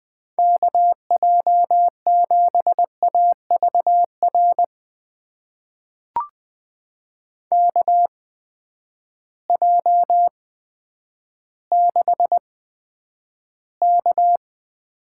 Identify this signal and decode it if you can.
Morse code, heard as a keyed tone, about 20 wpm, spaced as for 10 wpm, 700 Hz. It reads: KJ7AVR KJ6K
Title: USA Call Signs - 6 Characters - 8x Character Spacing - 20wpm